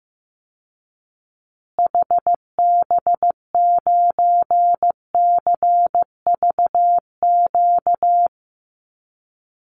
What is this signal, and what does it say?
Morse code, heard as a keyed tone, about 15 wpm, 700 Hz. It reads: HB9CVQ
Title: Call Signs Repeated 2x - 15wpm